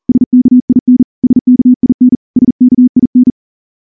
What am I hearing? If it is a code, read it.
Morse code, heard as a keyed tone, about 38 wpm, 250 Hz. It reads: SKIN SKIN SKIN
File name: morse-training-40WPM